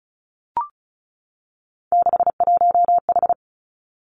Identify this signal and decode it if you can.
Morse code, heard as a keyed tone, about 36 wpm, 700 Hz. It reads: 61H
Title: Three Challenging Letter-Number Combinations 35wpm